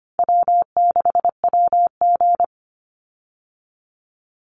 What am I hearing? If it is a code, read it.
Morse code, heard as a keyed tone, about 25 wpm, 700 Hz. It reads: W6WG
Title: Easy Call Signs - 25wpm